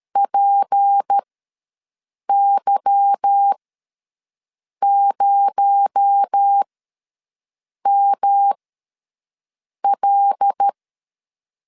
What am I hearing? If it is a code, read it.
Morse code, heard as a keyed tone, about 13 wpm, spaced as for 6 wpm, 800 Hz. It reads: PY0ML
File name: LEZIONI / 8 YJ45 / 8 Y ws 12wpm 3409